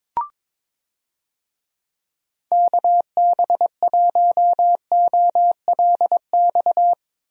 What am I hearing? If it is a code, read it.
Morse code, heard as a keyed tone, about 22 wpm, 700 Hz. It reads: KB1OLX
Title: USA Call Signs - 6 Characters - Repeated 2x - 22wpm